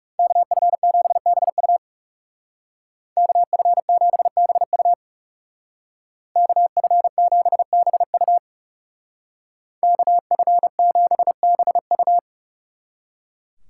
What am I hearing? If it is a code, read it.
Morse code, heard as a keyed tone, about 37 wpm, 700 Hz. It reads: KF7BU KF7BU KF7BU KF7BU
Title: Speed Racing - USA 2x2 Call Signs 30wpm